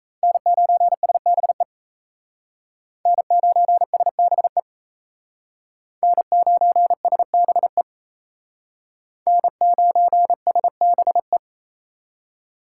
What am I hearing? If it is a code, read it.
Morse code, heard as a keyed tone, about 35 wpm, 700 Hz. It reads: N9SBE N9SBE N9SBE N9SBE
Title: Speed Racing - USA 1x3 Call Signs 28wpm